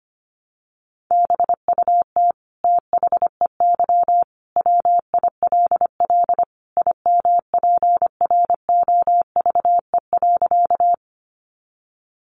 Morse code, heard as a keyed tone, about 25 wpm, 700 Hz. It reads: BUT THEY WILL IMPROVE.
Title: Sentences from Top 800 Words 25wpm